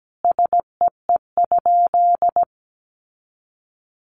Morse code, heard as a keyed tone, about 17 wpm, 700 Hz. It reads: SEE?
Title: Sentences from Top 100 Words 17wpm